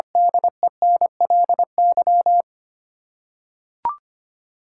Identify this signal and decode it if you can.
Morse code, heard as a keyed tone, about 25 wpm, 700 Hz. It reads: DENLY
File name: Eight Letter Words - 25wpm